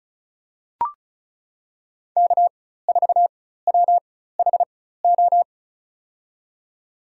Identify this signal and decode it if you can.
Morse code, heard as a keyed tone, about 36 wpm, spaced as for 16 wpm, 700 Hz. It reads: K4WHO
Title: USA Call Signs - 5 Characters - 4x Character Spacing - 35wpm